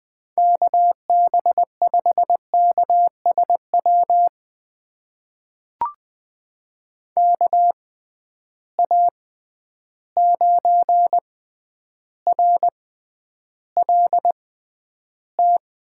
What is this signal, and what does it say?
Morse code, heard as a keyed tone, about 20 wpm, spaced as for 10 wpm, 700 Hz. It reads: KB5KSW KA9RLT
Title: USA Call Signs - 6 Characters - 6x Character Spacing - 20wpm